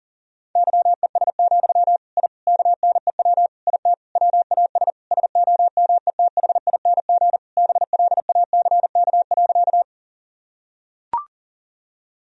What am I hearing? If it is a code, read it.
Morse code, heard as a keyed tone, about 40 wpm, 700 Hz. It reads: YES, I KNEW IT WAS SOMETHING BLACK.
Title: Sentences from Top 1000 Words - Review Part 2 - 40wpm